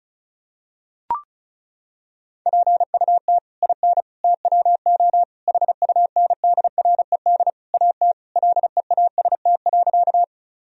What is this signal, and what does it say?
Morse code, heard as a keyed tone, about 34 wpm, 700 Hz. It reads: PUT IN TWO HUNDRED AT LEAST.
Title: Sentences from Top 1000 Words - Review Part 2 - 35wpm